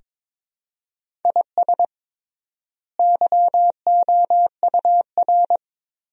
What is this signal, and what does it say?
Morse code, heard as a keyed tone, about 22 wpm, 700 Hz. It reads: IS YOUR
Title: 3x Word-Spacing - Sentences from Top 400 Words - 22wpm